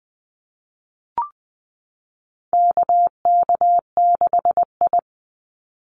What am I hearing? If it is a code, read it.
Morse code, heard as a keyed tone, about 20 wpm, 700 Hz. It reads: KK6I
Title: Call Signs Repeated 2x - 20wpm